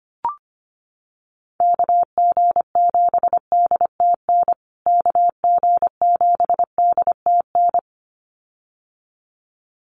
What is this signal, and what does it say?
Morse code, heard as a keyed tone, about 25 wpm, 700 Hz. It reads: KG7DTN KG7DTN
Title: USA Call Signs Repeated 2x 25wpm